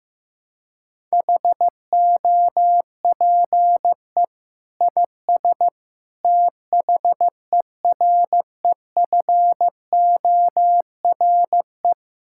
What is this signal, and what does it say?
Morse code, heard as a keyed tone, about 15 wpm, 700 Hz. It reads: HOPE IS THEREFORE
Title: Sentences from Top 1000 Words - Review Part 5 - 15wpm